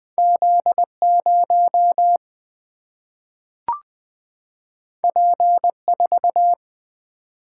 Morse code, heard as a keyed tone, about 20 wpm, 700 Hz. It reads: Z0 P4
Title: Single-Letter Single-Number 20wpm